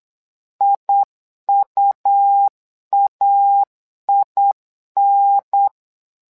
Morse code, heard as a keyed tone, about 8 wpm, 800 Hz. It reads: IUAIN
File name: LEZIONI / 2 UHDN / 2 UHDN 8wpm2232